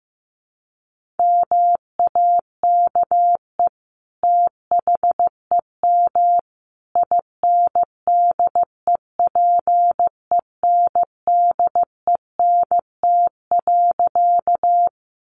Morse code, heard as a keyed tone, about 15 wpm, 700 Hz. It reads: MAKE THEM INDEPENDENT.
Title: Sentences from Top 900 Words 15wpm